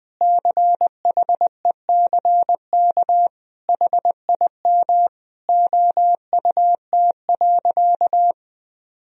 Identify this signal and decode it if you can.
Morse code, heard as a keyed tone, about 21 wpm, 700 Hz. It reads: CHECK HIM OUT.
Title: Sentences from Top 1000 Words - Review Part 3 - 20wpm